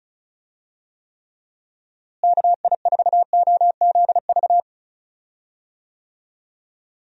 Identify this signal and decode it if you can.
Morse code, heard as a keyed tone, about 36 wpm, 700 Hz. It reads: KI4OZV
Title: USA Call Signs - 6 Characters - Repeated 3x - 35wpm